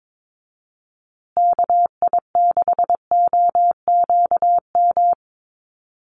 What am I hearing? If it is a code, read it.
Morse code, heard as a keyed tone, about 22 wpm, 700 Hz. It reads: KI6OQM
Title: USA Call Signs Repeated 3x 22wpm